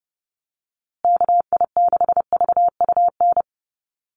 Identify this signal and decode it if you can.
Morse code, heard as a keyed tone, about 30 wpm, 700 Hz. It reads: KI6VUN
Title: USA Call Signs Repeated 2x 30wpm